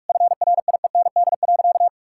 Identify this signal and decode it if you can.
Morse code, heard as a keyed tone, about 46 wpm, 700 Hz. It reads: FRIEND.
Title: Sentences from Top 1000 Words - Review Part 5 - 45wpm